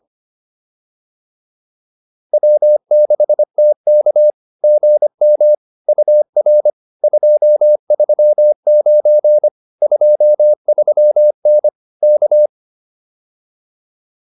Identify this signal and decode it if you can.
Morse code, heard as a keyed tone, about 25 wpm, 600 Hz. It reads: W6TK GM UR 239 23N K